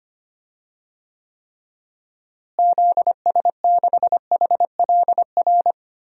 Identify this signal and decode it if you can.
Morse code, heard as a keyed tone, about 25 wpm, 700 Hz. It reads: ZS6HLR
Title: DXCC-100 Call Signs - 6 Characters - Repeated 3x - 25wpm